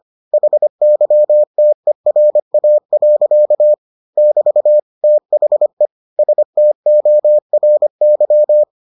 Morse code, heard as a keyed tone, about 25 wpm, 600 Hz. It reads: HYTERA. <BT> THE STORY